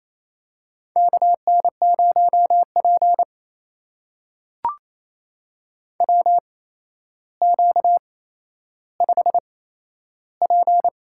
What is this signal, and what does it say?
Morse code, heard as a keyed tone, about 27 wpm, spaced as for 12 wpm, 700 Hz. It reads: KN0P WQ5P
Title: USA Call Signs - 4 Characters - 8x Character Spacing - 28wpm